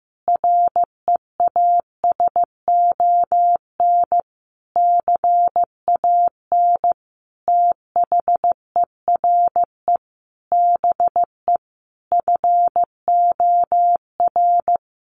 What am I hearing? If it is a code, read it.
Morse code, heard as a keyed tone, about 14 wpm, 700 Hz. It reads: REASON CAN THERE BE FOR
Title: Sentences from Top 1300 Words 15wpm